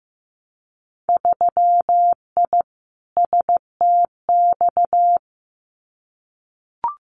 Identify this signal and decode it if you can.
Morse code, heard as a keyed tone, about 15 wpm, 700 Hz. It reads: 3I STX
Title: Winter Field Day Exchanges Encore 15wpm